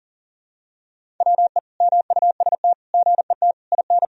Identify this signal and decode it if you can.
Morse code, heard as a keyed tone, about 40 wpm, 700 Hz. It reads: WE MUST GET IN
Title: Sentences from Top 1000 Words - Review Part 1 - 40wpm